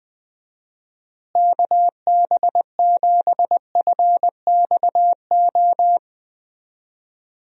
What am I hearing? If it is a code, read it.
Morse code, heard as a keyed tone, about 21 wpm, 700 Hz. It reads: KB7FXO
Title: USA Call Signs - 6 Characters - Repeated 2x - 20wpm